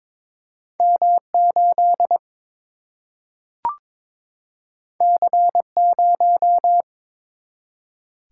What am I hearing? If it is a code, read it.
Morse code, heard as a keyed tone, about 22 wpm, 700 Hz. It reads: M8 C0
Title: Single-Letter Single-Number 22wpm